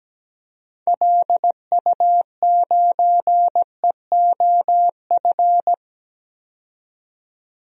Morse code, heard as a keyed tone, about 17 wpm, 700 Hz. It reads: LU9EOF
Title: DXCC-100 Call Signs - 6 Characters - 17wpm